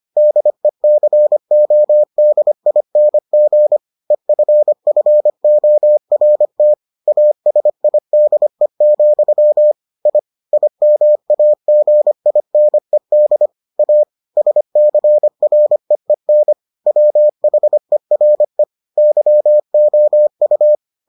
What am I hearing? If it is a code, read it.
Morse code, heard as a keyed tone, about 25 wpm, 600 Hz. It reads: DECODING EFFORT ASIDE, I IMAGINED A SCREEN WHERE YOU